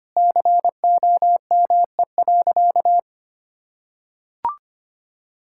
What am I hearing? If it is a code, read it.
Morse code, heard as a keyed tone, about 25 wpm, 700 Hz. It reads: COME.
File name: practice / 25wpm / Sentences with 2-Word Phrases - 25wpm